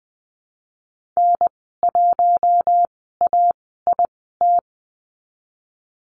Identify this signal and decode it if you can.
Morse code, heard as a keyed tone, about 20 wpm, spaced as for 14 wpm, 700 Hz. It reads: N1AIT
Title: 082 Lesson 14g - Number 2 - Call Signs 20wpm